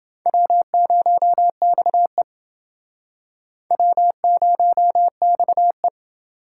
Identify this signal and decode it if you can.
Morse code, heard as a keyed tone, about 28 wpm, 700 Hz. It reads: W0XE W0XE